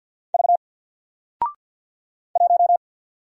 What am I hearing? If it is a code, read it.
Morse code, heard as a keyed tone, about 48 wpm, 700 Hz. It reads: V 1
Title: Single Letter-Number Alt 50wpm